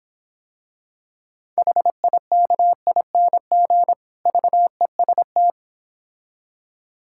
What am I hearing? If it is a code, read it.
Morse code, heard as a keyed tone, about 26 wpm, 700 Hz. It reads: HIKING VEST